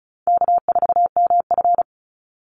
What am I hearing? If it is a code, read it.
Morse code, heard as a keyed tone, about 34 wpm, 700 Hz. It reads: K4MF
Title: NA SOTA Chaser Callsigns 35wpm